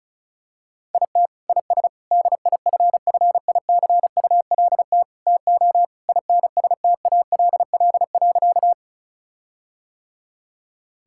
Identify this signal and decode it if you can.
Morse code, heard as a keyed tone, about 36 wpm, 700 Hz. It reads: IT IS DIFFICULT TO INSTALL.
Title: Sentences from Top 1600 Words 35wpm